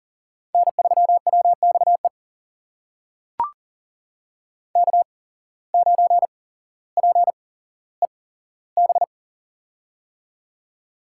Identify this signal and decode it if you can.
Morse code, heard as a keyed tone, about 40 wpm, spaced as for 17 wpm, 700 Hz. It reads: N3WXE K9PEB
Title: USA Call Signs - 5 Characters - 8x Character Spacing - 40wpm